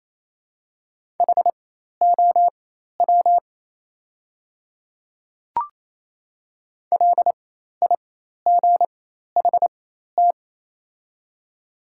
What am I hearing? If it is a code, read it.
Morse code, heard as a keyed tone, about 28 wpm, spaced as for 13 wpm, 690 Hz. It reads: HOW LIGHT
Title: 4x Character Spacing - Top 300 Words - 28wpm